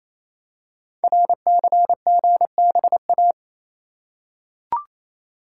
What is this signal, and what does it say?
Morse code, heard as a keyed tone, about 27 wpm, 700 Hz. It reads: RCGBA
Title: Code Groups - 5 Letters - 28wpm